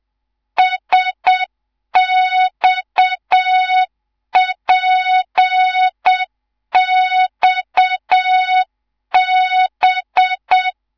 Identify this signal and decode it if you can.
Morse code, heard as a keyed tone, about 7 wpm, 3700 Hz. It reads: SXPXB
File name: Morze-HA2MN / hang / 11 GYAKORLAT